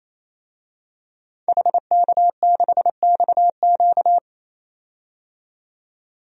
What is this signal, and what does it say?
Morse code, heard as a keyed tone, about 28 wpm, 700 Hz. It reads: HK6XQ
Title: DXCC-100 Call Signs - 5 Characters - 28wpm